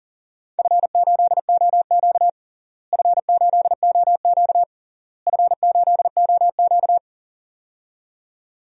Morse code, heard as a keyed tone, about 40 wpm, 700 Hz. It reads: F8OQ F8OQ F8OQ
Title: DXCC-100 Call Signs - 4 Characters - Repeated 3x - 40wpm